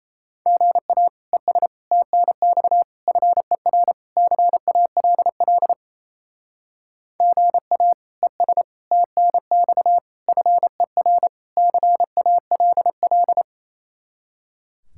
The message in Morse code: GA ES TNX FER CALL GA ES TNX FER CALL

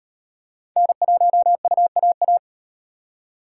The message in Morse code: N1UAA

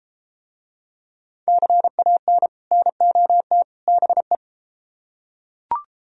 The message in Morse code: CAN NOT BE

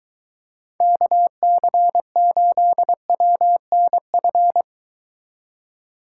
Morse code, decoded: KC8WNF

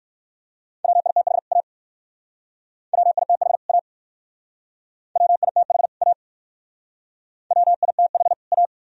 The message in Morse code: WITH A WITH A WITH A WITH A

